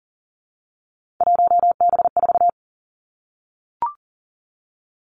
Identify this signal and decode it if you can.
Morse code, heard as a keyed tone, about 40 wpm, 700 Hz. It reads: 1B4
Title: Three Challenging Letter-Number Combinations 40wpm